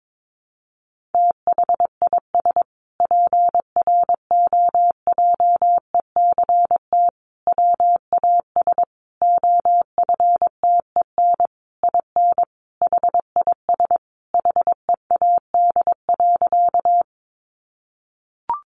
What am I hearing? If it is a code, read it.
Morse code, heard as a keyed tone, about 22 wpm, 700 Hz. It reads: THIS PROJECT WAS OFTEN IN HIS HEAD.
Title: Sentences from Top 1400 Words 22wpm